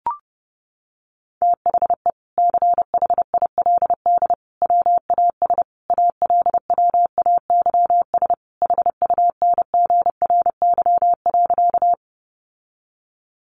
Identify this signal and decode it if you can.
Morse code, heard as a keyed tone, about 30 wpm, 700 Hz. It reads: THE CHILD WAS ALWAYS HUNGRY.
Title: Sentences from Top 1400 Words 30wpm